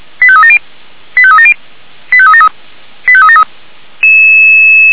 Five-tone selective call (ZVEI), heard as quick runs of five tones, tones of about 70 ms. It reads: 85370 85370 85373 85373